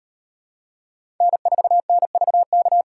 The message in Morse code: N4NVK

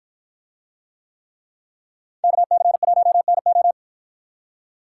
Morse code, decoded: KC1NO

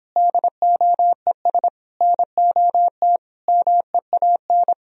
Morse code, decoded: DOES NOT MEAN

